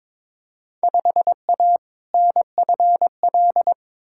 Morse code, heard as a keyed tone, about 22 wpm, 700 Hz. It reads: 5A NFL
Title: Field Day Exchanges - 22wpm